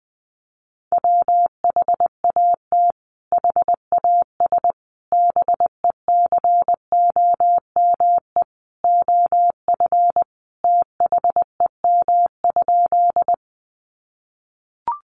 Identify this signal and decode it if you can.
Morse code, heard as a keyed tone, about 20 wpm, 700 Hz. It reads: WHAT HAS BECOME OF THEM?
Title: Sentences from Top 600 Words 20wpm